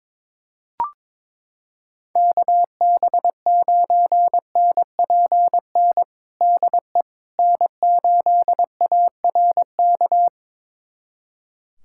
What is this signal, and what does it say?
Morse code, heard as a keyed tone, about 22 wpm, 700 Hz. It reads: KB9NPN DE N8ARK
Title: USA Callsign DE Callsign 22wpm